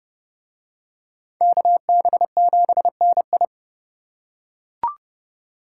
Morse code, heard as a keyed tone, about 28 wpm, 700 Hz. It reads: KB7NI